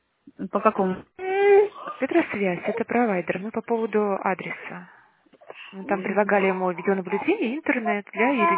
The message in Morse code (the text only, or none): none